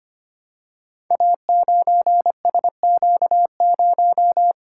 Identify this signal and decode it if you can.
Morse code, heard as a keyed tone, about 25 wpm, 700 Hz. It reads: A9SQ0
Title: Code Groups - 5 Letter:Numbers - 25wpm